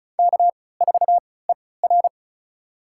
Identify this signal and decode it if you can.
Morse code, heard as a keyed tone, about 36 wpm, spaced as for 20 wpm, 700 Hz. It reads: K4ER
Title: USA Call Signs - 3x Character Spacing - 35wpm